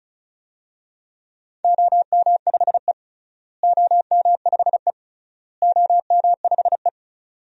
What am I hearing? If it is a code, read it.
Morse code, heard as a keyed tone, about 36 wpm, 700 Hz. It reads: OM5E OM5E OM5E
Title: DXCC-100 Call Signs - 4 Characters - Repeated 3x - 35wpm